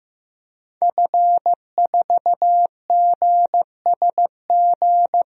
Morse code, heard as a keyed tone, about 15 wpm, 700 Hz. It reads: F4GSG
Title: DXCC-100 Call Signs - 5 Characters - Repeated 3x - 15wpm